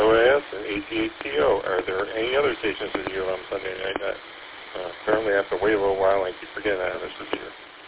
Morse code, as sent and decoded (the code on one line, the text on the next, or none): none